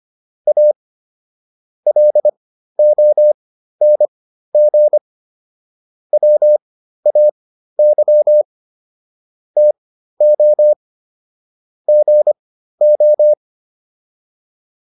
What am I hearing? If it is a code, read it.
Morse code, heard as a keyed tone, about 25 wpm, spaced as for 13 wpm, 600 Hz. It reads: A LONG WAY TO GO